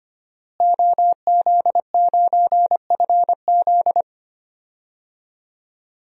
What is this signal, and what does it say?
Morse code, heard as a keyed tone, about 25 wpm, 700 Hz. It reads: OZ9FZ
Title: DXCC-100 Call Signs - 5 Characters - Repeated 3x - 25wpm